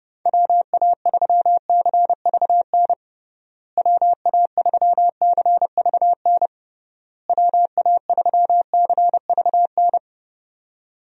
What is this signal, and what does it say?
Morse code, heard as a keyed tone, about 30 wpm, 700 Hz. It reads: WA3CVN WA3CVN WA3CVN
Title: USA Call Signs - 6 Characters - Repeated 3x - 30wpm